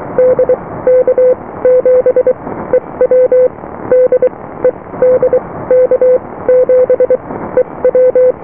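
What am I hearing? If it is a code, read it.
Morse code, heard as a keyed tone, about 23 wpm, spaced as for 16 wpm, 510 Hz. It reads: DK7EW DE DK7EW
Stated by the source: Morse QSO on 14 MHz with a German station, received at RST 579.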